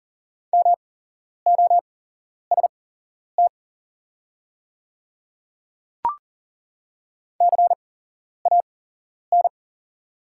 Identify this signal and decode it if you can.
Morse code, heard as a keyed tone, about 40 wpm, spaced as for 11 wpm, 700 Hz. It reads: MOST CAN
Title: 8x Character Spacing - Top 300 Words - 40wpm